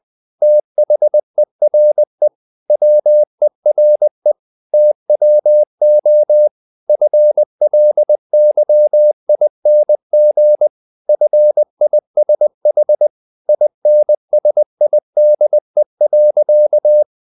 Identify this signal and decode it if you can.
Morse code, heard as a keyed tone, about 20 wpm, 600 Hz. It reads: THERE WERE TWO FLYING FISH INSIDE.
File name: the old man and the sea part 2 20wpm